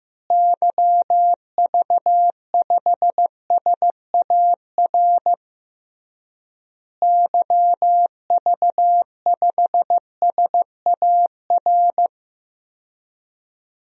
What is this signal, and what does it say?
Morse code, heard as a keyed tone, about 15 wpm, 700 Hz. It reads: YV5SAR YV5SAR
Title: DXCC-100 Call Signs - 6 Characters - Repeated 3x - 15wpm